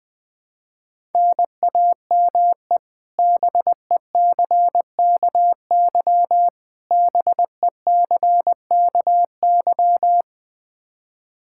Morse code, heard as a keyed tone, about 21 wpm, 700 Hz. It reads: NAME BECKY BECKY
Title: K1USN SST QSO Practice 20wpm